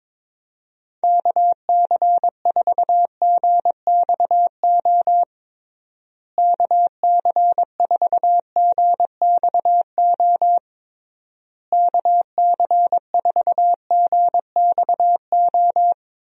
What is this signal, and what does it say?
Morse code, heard as a keyed tone, about 22 wpm, 700 Hz. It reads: KC4GXO KC4GXO KC4GXO
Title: USA Call Signs - 6 Characters - Repeated 3x - 22wpm